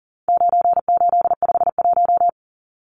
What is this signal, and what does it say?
Morse code, heard as a keyed tone, about 38 wpm, 700 Hz. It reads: 9851
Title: Numbers - 4-Digit 40wpm